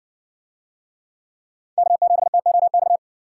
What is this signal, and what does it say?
Morse code, heard as a keyed tone, about 60 wpm, 700 Hz. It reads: K7TOX